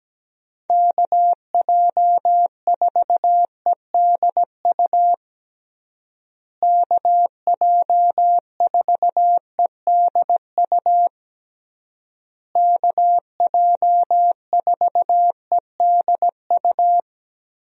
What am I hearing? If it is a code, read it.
Morse code, heard as a keyed tone, about 17 wpm, 700 Hz. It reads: KJ4EDU KJ4EDU KJ4EDU
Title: USA Call Signs - 6 Characters - Repeated 3x - 17wpm